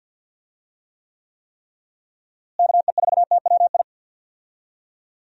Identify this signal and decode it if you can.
Morse code, heard as a keyed tone, about 48 wpm, 700 Hz. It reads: KE4TWI
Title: USA Call Signs - 6 Characters - Repeated 3x - 50wpm